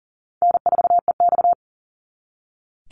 Morse code, heard as a keyed tone, about 40 wpm, 700 Hz. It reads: N4EX
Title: NA SOTA Chaser Callsigns 40wpm